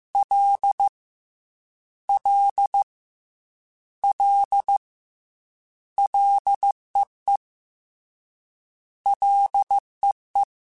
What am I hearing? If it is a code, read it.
Morse code, heard as a keyed tone, about 15 wpm, 800 Hz. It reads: L L L LEE LEE
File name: Combinations / Combinations-mp3s / Combinations@13